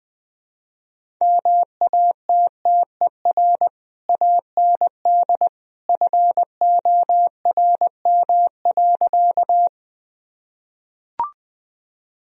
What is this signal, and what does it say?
Morse code, heard as a keyed tone, about 21 wpm, 700 Hz. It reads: MATTER AND FORM.
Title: Sentences from Top 1000 Words - Review Part 1 - 20wpm